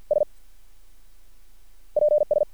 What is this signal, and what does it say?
Morse code, heard as a keyed tone, about 50 wpm, spaced as for 9 wpm, 600 Hz. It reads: SPS
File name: archived / CRSnet